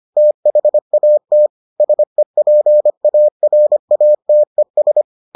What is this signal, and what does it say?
Morse code, heard as a keyed tone, about 25 wpm, 600 Hz. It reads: THAT SEPARATES